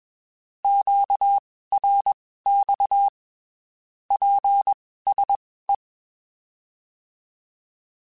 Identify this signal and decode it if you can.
Morse code, heard as a keyed tone, about 21 wpm, spaced as for 15 wpm, 800 Hz. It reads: QRX PSE
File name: (spaced) / QSOs Spaced-mp3s / kc7haa @20 sp